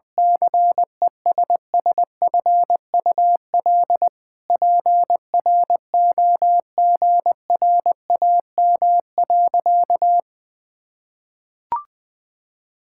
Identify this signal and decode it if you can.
Morse code, heard as a keyed tone, about 20 wpm, 700 Hz. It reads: CESSFUL PROGRAM.
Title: Sentences from Top 1000 Words - Review Part 5 - 20wpm